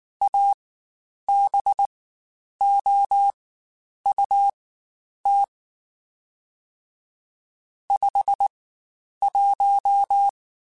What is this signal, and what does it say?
Morse code, heard as a keyed tone, about 19 wpm, spaced as for 9 wpm, 800 Hz. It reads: ABOUT 51
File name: Spaced Characters / Spaced Characters-mp3s / Weather 4 21 sp @18